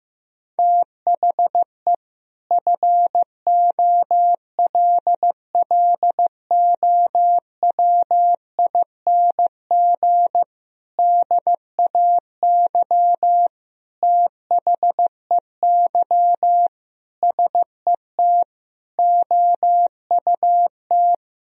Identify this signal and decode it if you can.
Morse code, heard as a keyed tone, about 15 wpm, 700 Hz. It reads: THE FOLLOWING DAY THEY SET OUT
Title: Sentences from Top 1000 Words - Review Part 4 - 15wpm